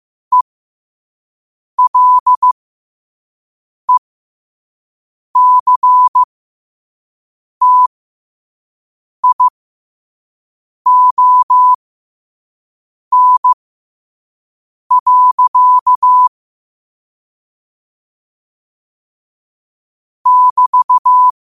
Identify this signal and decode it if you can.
Morse code, heard as a keyed tone, about 15 wpm, spaced as for 5 wpm, 1000 Hz. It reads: ELECTION. <BT>
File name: qotdfiles / 5wpm